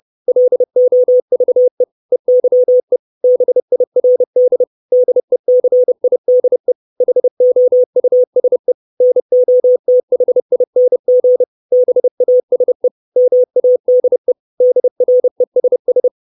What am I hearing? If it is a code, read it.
Morse code, heard as a keyed tone, about 30 wpm, 500 Hz. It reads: LOVE EYE BIRD DECIDE HOUSE NOTHING BASE MADE DRESS
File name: 632 Words 30WPM